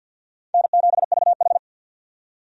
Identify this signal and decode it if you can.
Morse code, heard as a keyed tone, about 51 wpm, 700 Hz. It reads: N7VH
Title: USA Call Signs - 4 Characters - 50wpm